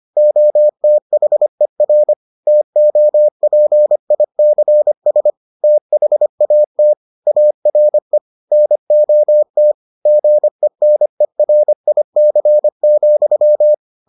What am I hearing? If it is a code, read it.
Morse code, heard as a keyed tone, about 25 wpm, 600 Hz. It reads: OTHER TOPICS THAT ARE NOT GENERIC,